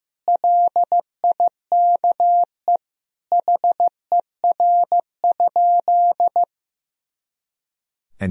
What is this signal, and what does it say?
Morse code, heard as a keyed tone, about 15 wpm, 700 Hz. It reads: LIKE HER?
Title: Sentences from Top 100 Words and Abbreviations 15wpm